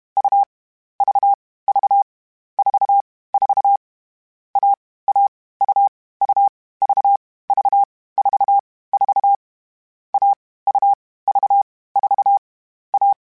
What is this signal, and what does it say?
Morse code, heard as a keyed tone, about 32 wpm, 800 Hz. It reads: U V V 4 4 A A U U V V 4 4 A U V 4 A
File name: CW Code Groups-mp3s / Sets mp3s / AUV AUV4 sp @30